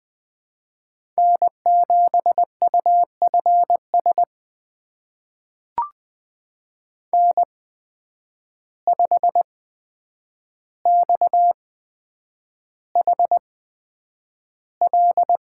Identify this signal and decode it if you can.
Morse code, heard as a keyed tone, about 20 wpm, spaced as for 9 wpm, 700 Hz. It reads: N7UFS N5XHL
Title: USA Call Signs - 5 Characters - 8x Character Spacing - 20wpm